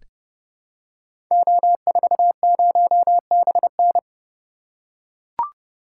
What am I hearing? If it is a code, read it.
Morse code, heard as a keyed tone, about 30 wpm, 700 Hz. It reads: O40BN
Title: Code Groups - 5 Letter:Numbers - 30wpm